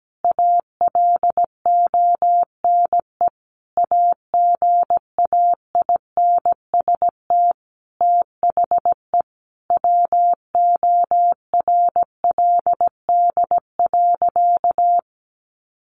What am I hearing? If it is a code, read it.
Morse code, heard as a keyed tone, about 17 wpm, 700 Hz. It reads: ALONE AGAINST THE WORLD.
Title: Sentences from Top 1400 Words 17wpm